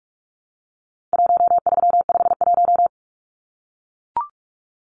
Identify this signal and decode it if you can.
Morse code, heard as a keyed tone, about 44 wpm, 700 Hz. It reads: 1351
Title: Numbers - 4-Digit 45wpm